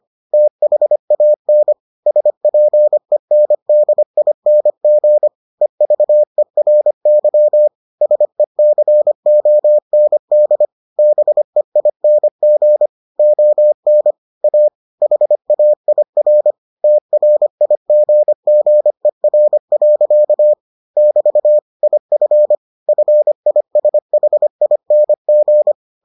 Morse code, heard as a keyed tone, about 25 wpm, 600 Hz. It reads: THAN SPENDING EVERY SECOND BEING ON A HAIR TRIGGER. <BT> IF FISHING